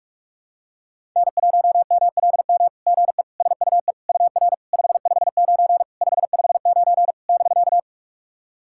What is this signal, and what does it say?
Morse code, heard as a keyed tone, about 44 wpm, 690 Hz. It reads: N1MLM GE SUE UR 559 559 <BK>
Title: SOTA QSO Practice 45wpm